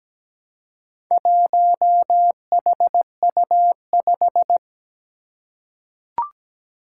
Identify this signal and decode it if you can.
Morse code, heard as a keyed tone, about 17 wpm, 700 Hz. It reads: 1HU5
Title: Four Challenging Letter-Number Combinations 17wpm